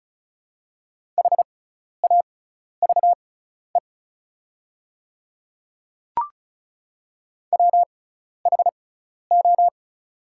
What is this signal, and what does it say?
Morse code, heard as a keyed tone, about 36 wpm, spaced as for 12 wpm, 700 Hz. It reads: HAVE WHO
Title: 6x Character Spacing - Top 100 Words - 35wpm